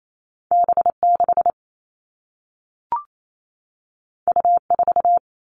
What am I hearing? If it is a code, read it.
Morse code, heard as a keyed tone, about 28 wpm, 700 Hz. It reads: B6 U4
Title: All Cut Numbers - 2-Digit 28wpm